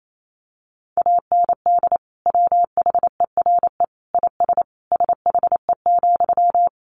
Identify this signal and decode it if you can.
Morse code, heard as a keyed tone, about 28 wpm, 700 Hz. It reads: AND WHERE IS SHE,
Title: Sentences from Top 200 Words 28wpm